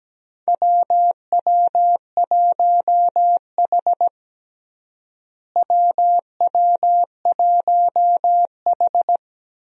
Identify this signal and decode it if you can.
Morse code, heard as a keyed tone, about 17 wpm, 700 Hz. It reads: WW1H WW1H
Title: USA Call Signs - 4 Characters - Repeated 3x - 17wpm